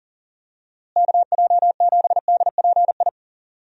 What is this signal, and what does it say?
Morse code, heard as a keyed tone, about 40 wpm, 700 Hz. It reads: KJ7DPI
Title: USA Call Signs - 6 Characters - Repeated 3x - 40wpm